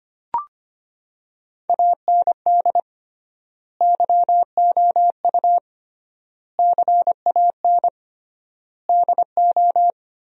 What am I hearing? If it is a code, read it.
Morse code, heard as a keyed tone, about 25 wpm, 700 Hz. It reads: AND YOU CAN DO
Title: 3x Word-Spacing - Sentences from Top 500 Words - 25wpm